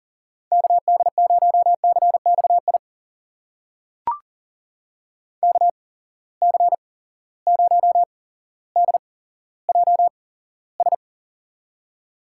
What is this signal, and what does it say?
Morse code, heard as a keyed tone, about 40 wpm, spaced as for 17 wpm, 700 Hz. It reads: KD0CXI KC0DJS
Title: USA Call Signs - 6 Characters - 8x Character Spacing - 40wpm